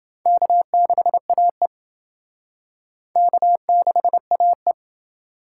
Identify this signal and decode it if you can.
Morse code, heard as a keyed tone, about 29 wpm, 700 Hz. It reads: K6AE K6AE